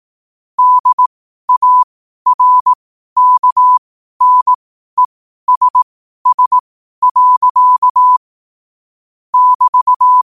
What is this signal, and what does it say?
Morse code, heard as a keyed tone, about 18 wpm, spaced as for 12 wpm, 1000 Hz. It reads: DARKNESS. <BT>
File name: qotdfiles / 12wpm